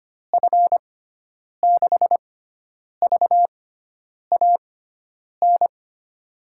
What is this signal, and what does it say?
Morse code, heard as a keyed tone, about 25 wpm, spaced as for 9 wpm, 700 Hz. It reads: F6VAN
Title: USA Call Signs - 6 Characters - 6x Character Spacing - 25wpm